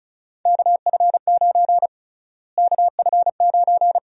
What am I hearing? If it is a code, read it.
Morse code, heard as a keyed tone, about 36 wpm, 700 Hz. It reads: KF9 KF9
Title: USA Call Sign Prefixes - Repeated 2x - 35wpm